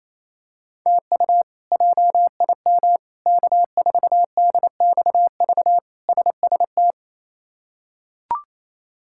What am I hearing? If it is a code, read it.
Morse code, heard as a keyed tone, about 28 wpm, 700 Hz. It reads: TU JIM K4DXV SST
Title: K1USN SST QSO Practice - 28wpm